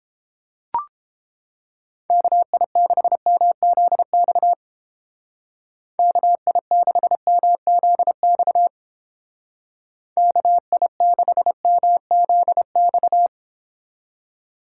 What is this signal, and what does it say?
Morse code, heard as a keyed tone, about 29 wpm, 700 Hz. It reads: KI6MZX KI6MZX KI6MZX